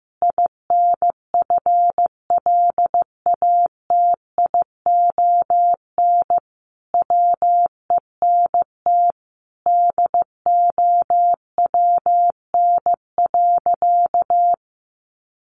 Morse code, heard as a keyed tone, about 15 wpm, 700 Hz. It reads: INFLATION WENT DOWN.
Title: Sentences from Top 1100 Words 15wpm